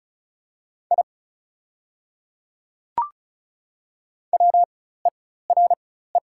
Morse code, heard as a keyed tone, about 33 wpm, spaced as for 16 wpm, 690 Hz. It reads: I WERE
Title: 4x Character Spacing - Top 200 Words - 35wpm